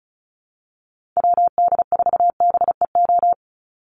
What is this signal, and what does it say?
Morse code, heard as a keyed tone, about 34 wpm, 700 Hz. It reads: WD4BEO
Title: USA Call Signs 35wpm